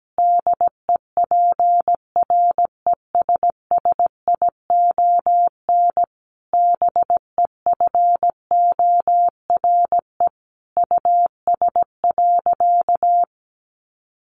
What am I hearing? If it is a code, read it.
Morse code, heard as a keyed tone, about 17 wpm, 700 Hz. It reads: DEPRESSION BEFORE US.
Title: Sentences from Top 1300 Words 17wpm